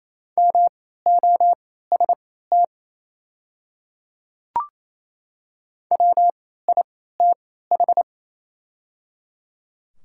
Morse code, heard as a keyed tone, about 27 wpm, spaced as for 16 wpm, 690 Hz. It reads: MOST WITH
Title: 3x Character Spacing - Top 100 Words - 28wpm